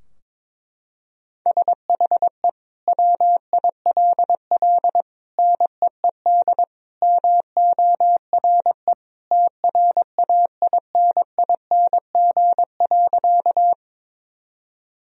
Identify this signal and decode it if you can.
Morse code, heard as a keyed tone, about 23 wpm, 700 Hz. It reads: SHE WILL NEED MORE TRAINING.
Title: Sentences from Top 1000 Words - Review Part 4 - 22wpm